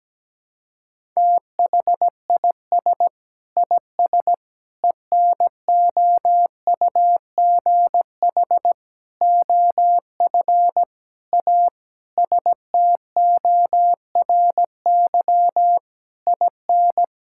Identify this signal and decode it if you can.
Morse code, heard as a keyed tone, about 17 wpm, 700 Hz. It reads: THIS IS ENOUGH OF A STORY IN
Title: Sentences from Top 1000 Words - Review Part 5 - 17wpm